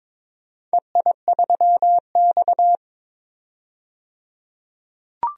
Morse code, heard as a keyed tone, about 22 wpm, 690 Hz. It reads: EI3X
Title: DXCC-100 Call Signs - 4 Characters - 22wpm